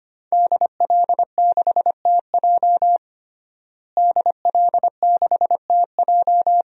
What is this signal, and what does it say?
Morse code, heard as a keyed tone, about 25 wpm, 700 Hz. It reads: DL6TJ DL6TJ
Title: DXCC-100 Call Signs - 5 Characters - Repeated 3x - 25wpm